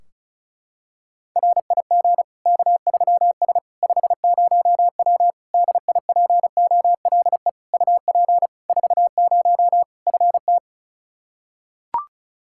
Morse code, heard as a keyed tone, about 34 wpm, 700 Hz. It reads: RIG K3S 50W DIPOLE UP 40 FT
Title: K1USN SST QSO Practice 35wpm